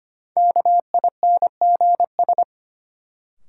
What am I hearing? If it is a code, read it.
Morse code, heard as a keyed tone, about 25 wpm, 700 Hz. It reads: KINGS
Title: Sentences from Top 1000 Words - Review Part 6 - New Word Drill - 25wpm